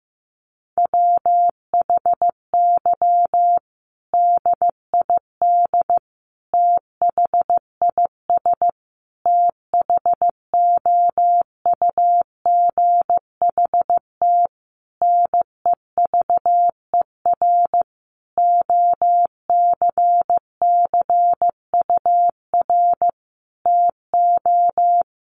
Morse code, heard as a keyed tone, about 15 wpm, 700 Hz. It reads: WHY DID THIS THOUGHT NEVER OCCUR TO
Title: Sentences from Top 900 Words 15wpm